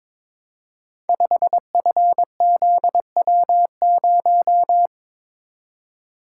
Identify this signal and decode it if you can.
Morse code, heard as a keyed tone, about 22 wpm, 690 Hz. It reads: 5FZW0
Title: Code Groups - 5 Letter:Numbers - 22wpm